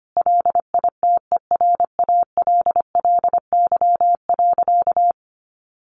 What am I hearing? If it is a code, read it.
Morse code, heard as a keyed tone, about 25 wpm, 700 Hz. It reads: LITERALLY.
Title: Sentences from Top 1400 Words 25wpm